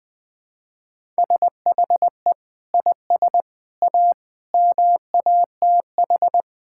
Morse code, heard as a keyed tone, about 20 wpm, 700 Hz. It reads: SHE IS A MATH
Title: Sentences from Top 1000 Words - Review Part 6 - 20wpm